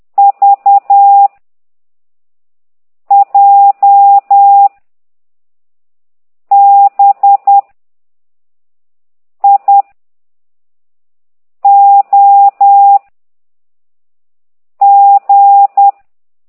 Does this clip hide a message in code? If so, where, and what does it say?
Yes: Morse code from 0.2 to 15.9 s, heard as a keyed tone, about 10 wpm, 800 Hz. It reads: V J B I O G